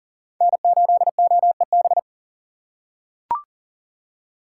Morse code, heard as a keyed tone, about 40 wpm, 700 Hz. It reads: N8OEB
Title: USA Call Signs - 5 Characters - 4x Character Spacing - 40wpm